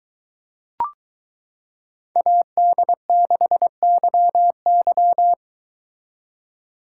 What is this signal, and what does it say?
Morse code, heard as a keyed tone, about 22 wpm, 700 Hz. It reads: AD6YY